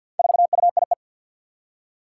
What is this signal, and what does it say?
Morse code, heard as a keyed tone, about 47 wpm, 700 Hz. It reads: 4UIE
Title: Four Challenging Letter-Number Combinations 50wpm